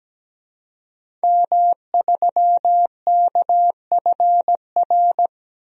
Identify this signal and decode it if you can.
Morse code, heard as a keyed tone, about 17 wpm, 700 Hz. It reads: M3KFR
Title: DXCC-100 Call Signs - 5 Characters - Repeated 2x - 17wpm